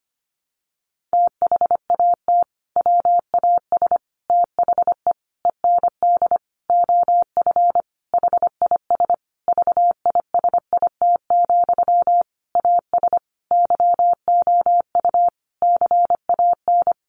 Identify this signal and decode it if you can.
Morse code, heard as a keyed tone, about 25 wpm, 700 Hz. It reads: THAT WAS THE END OF HIS VISIT, AS YOU CAN
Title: Sentences from Top 1400 Words 25wpm